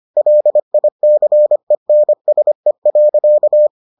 Morse code, heard as a keyed tone, about 25 wpm, 600 Hz. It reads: LICENSE.